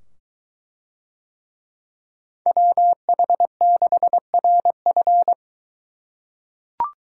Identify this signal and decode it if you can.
Morse code, heard as a keyed tone, about 23 wpm, 700 Hz. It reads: WH6RF